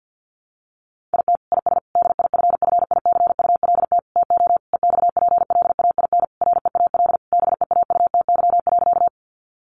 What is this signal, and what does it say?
Morse code, heard as a keyed tone, about 50 wpm, 700 Hz. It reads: IT IS DIFFICULT TO EXPLAIN REAL BEAUTY.
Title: Sentences from Top 1100 Words 50wpm